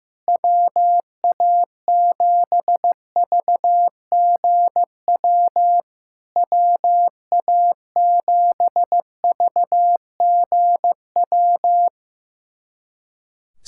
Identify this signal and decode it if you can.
Morse code, heard as a keyed tone, about 15 wpm, 700 Hz. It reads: WA7VGW WA7VGW
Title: SOTA QSO Practice 15wpm